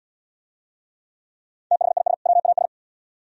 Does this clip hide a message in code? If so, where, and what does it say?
Yes: Morse code from 1.7 to 2.7 s, heard as a keyed tone, about 68 wpm, 700 Hz. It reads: THIS WAS